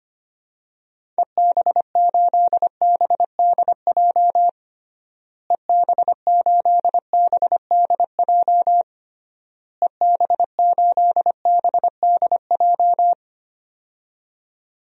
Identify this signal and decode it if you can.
Morse code, heard as a keyed tone, about 25 wpm, 700 Hz. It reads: EB8BDJ EB8BDJ EB8BDJ
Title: DXCC-100 Call Signs - 6 Characters - Repeated 3x - 25wpm